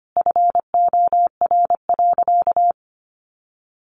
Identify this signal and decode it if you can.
Morse code, heard as a keyed tone, about 25 wpm, 700 Hz. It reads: FOR.
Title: Sentences from Top 1000 Words 25wpm